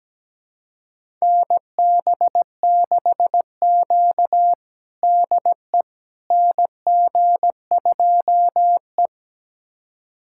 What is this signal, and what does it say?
Morse code, heard as a keyed tone, about 17 wpm, 700 Hz. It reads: NB6Q DE NG2E
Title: USA Callsign DE Callsign - 4 Characters - 17wpm